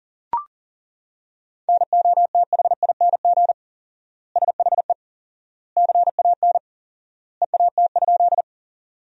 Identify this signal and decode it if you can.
Morse code, heard as a keyed tone, about 40 wpm, 700 Hz. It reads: NOTHING SHE CAN EAT?
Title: 4x Word-Spacing - Sentences from Top 500 Words - 40wpm